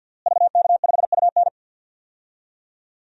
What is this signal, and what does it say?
Morse code, heard as a keyed tone, about 51 wpm, 700 Hz. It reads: VK5UN
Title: DXCC-100 Call Signs - 5 Characters - Repeated 3x - 50wpm